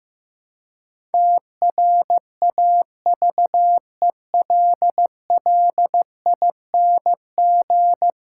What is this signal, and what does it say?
Morse code, heard as a keyed tone, about 15 wpm, 700 Hz. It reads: TRAVELLING
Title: Pangram Challenge v2 - 15wpm